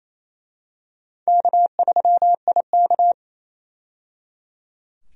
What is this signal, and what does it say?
Morse code, heard as a keyed tone, about 28 wpm, 700 Hz. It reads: K3IK
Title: USA Call Signs - 4 Characters - 28wpm